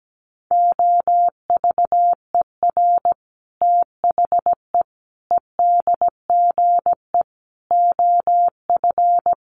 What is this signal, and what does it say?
Morse code, heard as a keyed tone, about 17 wpm, 700 Hz. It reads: OVER THE EDGE OF
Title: Sets of 5 Words Encore2 17wpm